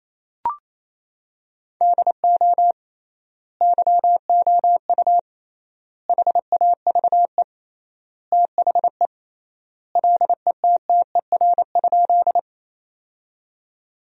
Morse code, heard as a keyed tone, about 28 wpm, 700 Hz. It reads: DO YOU HAVE THE LETTER?
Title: 3x Word-Spacing - Sentences from Top 300 Words - 28wpm